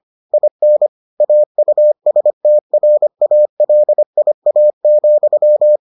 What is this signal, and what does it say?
Morse code, heard as a keyed tone, about 25 wpm, 600 Hz. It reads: IN AUSTRALIA,